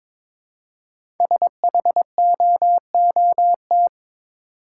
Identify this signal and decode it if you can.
Morse code, heard as a keyed tone, about 22 wpm, 700 Hz. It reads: SHOOT
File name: 5 Letter Words - First 3 Letters Given - 22wpm